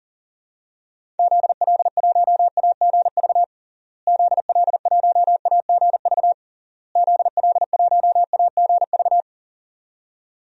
Morse code, heard as a keyed tone, about 37 wpm, 700 Hz. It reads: ZL1AGV ZL1AGV ZL1AGV
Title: DXCC-100 Call Signs - 6 Characters - Repeated 3x - 40wpm